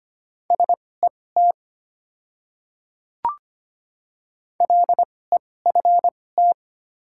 Morse code, heard as a keyed tone, about 25 wpm, spaced as for 18 wpm, 700 Hz. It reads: SET LEFT
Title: 2x Character Spacing - Top 300 Words - 25wpm